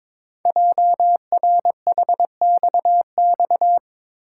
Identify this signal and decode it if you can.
Morse code, heard as a keyed tone, about 22 wpm, 700 Hz. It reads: JRHXX
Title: Code Groups - 5 Letters - 22wpm